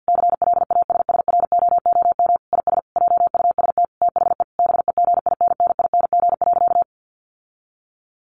Morse code, heard as a keyed tone, about 50 wpm, 700 Hz. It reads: CLASSROOM IS JUST THE BEGINNING.
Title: Sentences from Top 1500 Words 50wpm